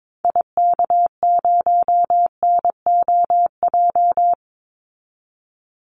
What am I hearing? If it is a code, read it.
Morse code, heard as a keyed tone, about 22 wpm, 700 Hz. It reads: IK0NOJ
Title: Call Signs Repeated 2x - 22wpm